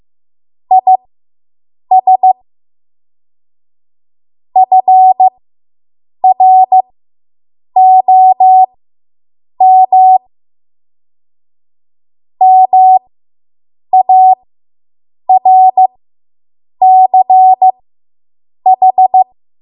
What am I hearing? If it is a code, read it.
Morse code, heard as a keyed tone, about 15 wpm, spaced as for 7 wpm, 750 Hz. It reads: IS FROM MARCH